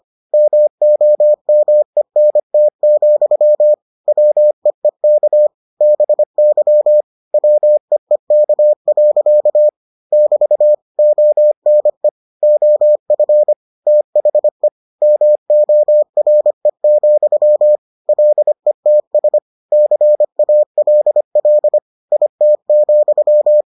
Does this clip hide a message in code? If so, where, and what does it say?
Yes: Morse code from 0.3 to 23.6 s, heard as a keyed tone, about 25 wpm, 600 Hz. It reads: MOMENT, WEEK BY WEEK. <BT> ONE OF THE MORE, LETS CALL IT,